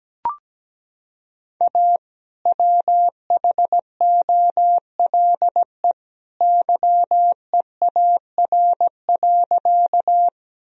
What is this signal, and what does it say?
Morse code, heard as a keyed tone, about 17 wpm, 700 Hz. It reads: A WHOLE YEAR.
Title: Sentences from Top 1000 Words - Review Part 1 - 17wpm